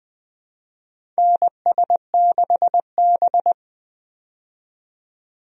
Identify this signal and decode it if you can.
Morse code, heard as a keyed tone, about 20 wpm, 700 Hz. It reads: NS6B
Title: USA Call Signs - 4 Characters - 20wpm